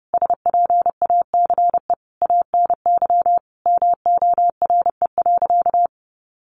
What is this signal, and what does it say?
Morse code, heard as a keyed tone, about 30 wpm, 700 Hz. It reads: SPACE ANY MORE.
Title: Sentences from Top 1500 Words 30wpm